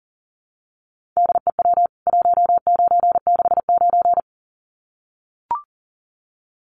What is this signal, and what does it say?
Morse code, heard as a keyed tone, about 40 wpm, 700 Hz. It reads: DEW 1969
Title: CWT Exchange 40wpm